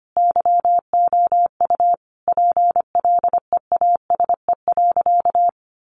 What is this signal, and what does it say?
Morse code, heard as a keyed tone, about 25 wpm, 700 Hz. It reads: YOU PLEASE.
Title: Sentences from Top 1500 Words 25wpm